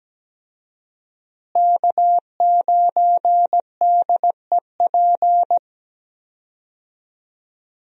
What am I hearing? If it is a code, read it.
Morse code, heard as a keyed tone, about 17 wpm, 700 Hz. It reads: K9DEP
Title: USA Call Signs - 5 Characters - Repeated 3x - 17wpm